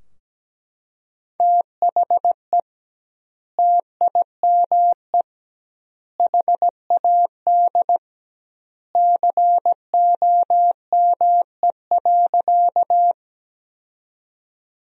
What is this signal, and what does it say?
Morse code, heard as a keyed tone, about 17 wpm, 700 Hz. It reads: THE TIME HAD COME.
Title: 2x Word-Spacing - Sentences from Top 100 Words - 17wpm